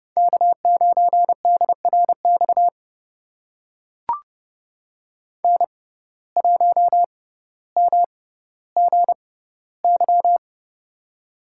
Morse code, heard as a keyed tone, about 32 wpm, spaced as for 16 wpm, 700 Hz. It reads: K9DRX N1MGY
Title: USA Call Signs - 5 Characters - 6x Character Spacing - 30wpm